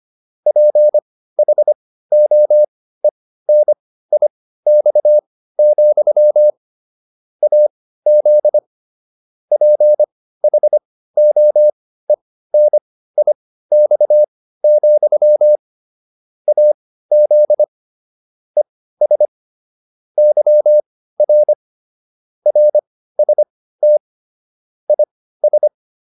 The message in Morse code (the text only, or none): PHOENIX, AZ PHOENIX, AZ ES YR RST IS